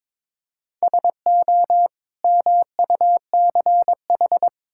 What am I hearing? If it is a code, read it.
Morse code, heard as a keyed tone, about 22 wpm, 700 Hz. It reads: SO MUCH
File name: Build A Phrase - 2-Word Phrases - 22wpm